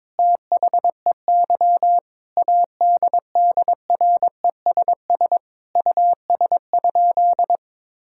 Morse code, heard as a keyed tone, about 22 wpm, 700 Hz. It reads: THEY ADDRESS US?
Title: Sentences from Top 1000 Words - Review Part 6 - 22wpm